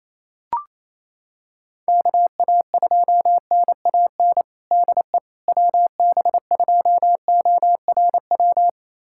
Morse code, heard as a keyed tone, about 28 wpm, 700 Hz. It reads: KA2NAN DE WB2ORW
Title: USA Callsign DE Callsign 28wpm